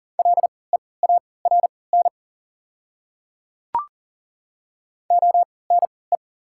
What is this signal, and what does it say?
Morse code, heard as a keyed tone, about 40 wpm, spaced as for 23 wpm, 690 Hz. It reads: LEARN ONE
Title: 3x Character Spacing - Top 300 Words - 40wpm